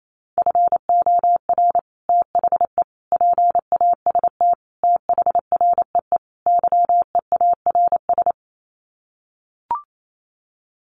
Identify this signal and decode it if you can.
Morse code, heard as a keyed tone, about 28 wpm, 700 Hz. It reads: FOR THE PAST THREE YEARS
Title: Sets of 5 Words Encore2 28wpm